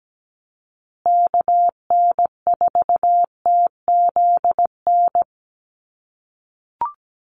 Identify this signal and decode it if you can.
Morse code, heard as a keyed tone, about 17 wpm, 700 Hz. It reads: KN4TZN
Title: USA Call Signs Repeated 2x 17wpm